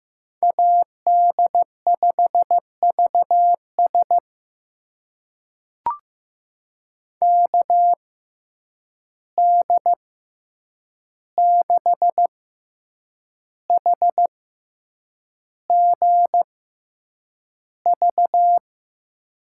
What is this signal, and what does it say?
Morse code, heard as a keyed tone, about 15 wpm, spaced as for 7 wpm, 700 Hz. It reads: AD5VS KD6HGV
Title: USA Call Signs - 6x Character Spacing - 15wpm